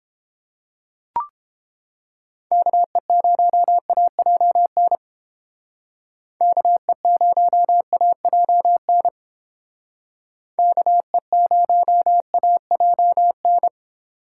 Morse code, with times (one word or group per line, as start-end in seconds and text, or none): 2.5-5.0 KE0AJN
6.4-9.1 KE0AJN
10.6-13.7 KE0AJN